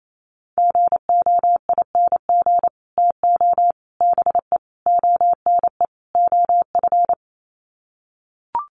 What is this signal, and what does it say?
Morse code, heard as a keyed tone, about 28 wpm, 700 Hz. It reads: GOING TO BE ONE OF
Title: Sets of 5 Words Encore2 28wpm